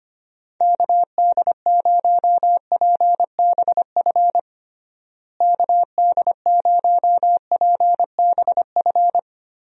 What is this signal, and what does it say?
Morse code, heard as a keyed tone, about 25 wpm, 700 Hz. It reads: KD0PBF KD0PBF
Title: USA Call Signs - 6 Characters - Repeated 3x - 25wpm